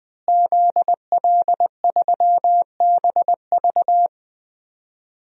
Morse code, heard as a keyed tone, about 20 wpm, 700 Hz. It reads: ZL3BV
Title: DXCC-100 Call Signs - 5 Characters - Repeated 3x - 20wpm